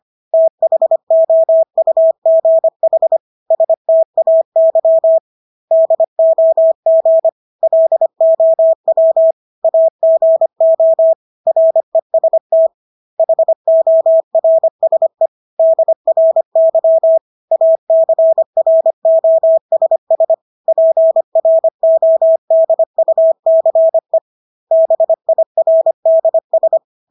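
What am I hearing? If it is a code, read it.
Morse code, heard as a keyed tone, about 25 wpm, 650 Hz. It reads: THOUGH STAY DOG LOW AGO REST HORSE DRY ACROSS PRODUCE BIRDS
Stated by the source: Morse practice of English words at 25 wpm from CW med Gnister.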